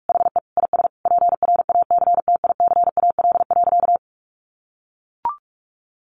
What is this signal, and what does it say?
Morse code, heard as a keyed tone, about 44 wpm, 700 Hz. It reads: HE IS PRACTICAL.
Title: Sentences from Top 1400 Words 45wpm